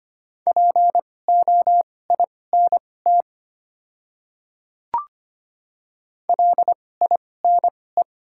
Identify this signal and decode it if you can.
Morse code, heard as a keyed tone, about 25 wpm, spaced as for 18 wpm, 700 Hz. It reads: POINT LINE
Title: 2x Character Spacing - Top 200 Words - 25wpm